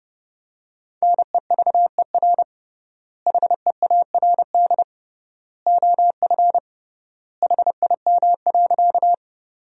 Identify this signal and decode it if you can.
Morse code, heard as a keyed tone, about 30 wpm, 700 Hz. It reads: NEVER HEARD OF HIM.
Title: 3x Word-Spacing - Sentences from Top 400 Words - 30wpm